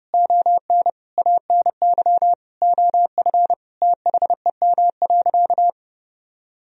Morse code, heard as a keyed tone, about 30 wpm, 700 Hz. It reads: ON ANY OF THEM.
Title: Sentences from Top 1000 Words - Review Part 1 - 30wpm